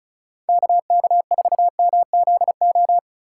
Morse code, heard as a keyed tone, about 36 wpm, 700 Hz. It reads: KK4MZO
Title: USA Call Signs - 6 Characters - 8x Character Spacing - 35wpm